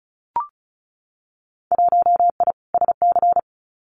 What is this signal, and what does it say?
Morse code, heard as a keyed tone, about 36 wpm, 700 Hz. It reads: 1I SC
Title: Winter Field Day Exchanges 35wpm